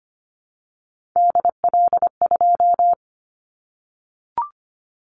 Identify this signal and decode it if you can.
Morse code, heard as a keyed tone, about 25 wpm, 700 Hz. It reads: DL2
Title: Callsign Prefixes for Top 50 DXCC - 25wpm